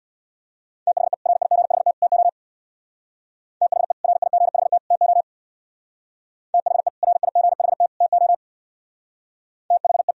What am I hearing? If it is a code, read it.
Morse code, heard as a keyed tone, about 60 wpm, 700 Hz. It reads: THE RIGHT TO THE RIGHT TO THE RIGHT TO THE
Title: Speed-Racing - Sets of 3 Words 50wpm